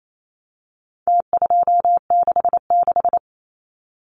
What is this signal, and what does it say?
Morse code, heard as a keyed tone, about 28 wpm, 700 Hz. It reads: T266
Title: Numbers - Contest Style - 4-Digit 28wpm